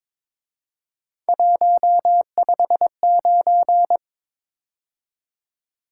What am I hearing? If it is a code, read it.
Morse code, heard as a keyed tone, about 26 wpm, 700 Hz. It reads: 159
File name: Speed-Racing - Three Letter-Number Combinations - 22wpm